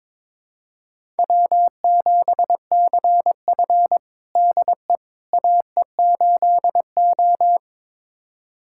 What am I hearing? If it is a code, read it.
Morse code, heard as a keyed tone, about 22 wpm, 700 Hz. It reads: W7CF DE AE8O
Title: USA Callsign DE Callsign - 4 Characters - 22wpm